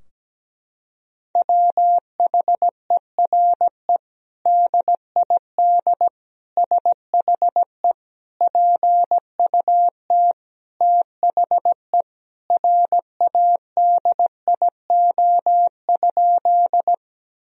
Morse code, heard as a keyed tone, about 17 wpm, 700 Hz. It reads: WHERE DID SHE PUT THE RADIO?
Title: Sentences from Top 1000 Words - Review Part 3 - 17wpm